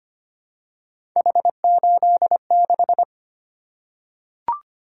Morse code, heard as a keyed tone, about 25 wpm, 700 Hz. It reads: H86
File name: Three Letter-Number Combinations - 25wpm